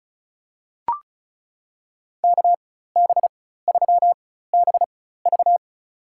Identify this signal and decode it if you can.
Morse code, heard as a keyed tone, about 36 wpm, spaced as for 16 wpm, 700 Hz. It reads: KB3BV
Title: USA Call Signs - 5 Characters - 4x Character Spacing - 35wpm